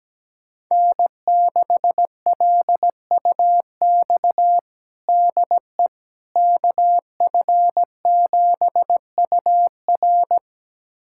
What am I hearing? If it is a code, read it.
Morse code, heard as a keyed tone, about 17 wpm, 700 Hz. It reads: N6LUX DE KF7UR
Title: USA Callsign DE Callsign - 5 Characters - 17wpm